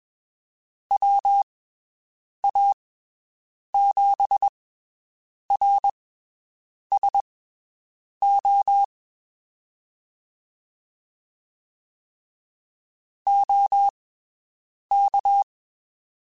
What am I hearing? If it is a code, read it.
Morse code, heard as a keyed tone, about 22 wpm, spaced as for 7 wpm, 800 Hz. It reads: WA7RSO OK
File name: (spaced) / QSOs Spaced-mp3s / WD0FBT 1 @20 sp